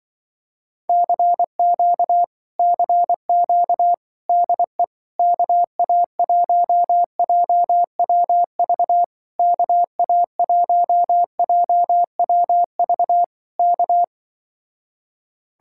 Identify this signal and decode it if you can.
Morse code, heard as a keyed tone, about 23 wpm, 700 Hz. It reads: CQ CQ DE KA1JWV KA1JWV K
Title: Speed-Racing Common QSO Phrases Encore - 20wpm